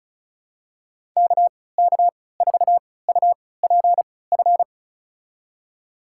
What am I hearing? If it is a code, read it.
Morse code, heard as a keyed tone, about 36 wpm, spaced as for 20 wpm, 700 Hz. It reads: KK4UPF
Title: USA Call Signs - 6 Characters - 3x Character Spacing - 35wpm